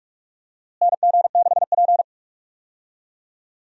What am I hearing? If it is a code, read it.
Morse code, heard as a keyed tone, about 47 wpm, 700 Hz. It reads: NG6P
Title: USA Call Signs - 4 Characters - Repeated 3x - 45wpm